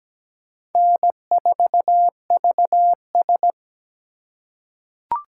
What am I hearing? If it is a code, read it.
Morse code, heard as a keyed tone, about 17 wpm, 700 Hz. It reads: N4VS
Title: USA Call Signs - 4 Characters - 2x Character Spacing - 17wpm